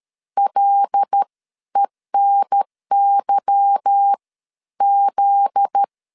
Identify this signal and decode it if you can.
Morse code, heard as a keyed tone, about 13 wpm, 800 Hz. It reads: LENY Z
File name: LEZIONI / 8 YJ45 / 8 Y 12wpm 2228